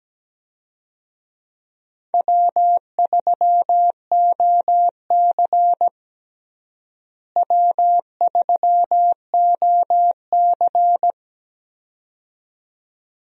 Morse code, heard as a keyed tone, about 17 wpm, 700 Hz. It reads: W3OC W3OC
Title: USA Call Signs - 4 Characters - Repeated 2x - 17wpm